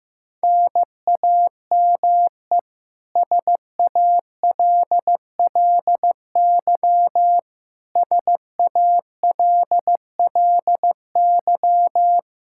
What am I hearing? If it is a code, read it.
Morse code, heard as a keyed tone, about 16 wpm, 700 Hz. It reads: NAME SALLY SALLY
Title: K1USN SST QSO Practice 15wpm